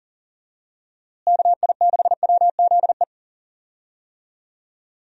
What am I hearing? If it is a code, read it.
Morse code, heard as a keyed tone, about 40 wpm, 700 Hz. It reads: KI6WZE
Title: USA Call Signs - 6 Characters - Repeated 2x - 40wpm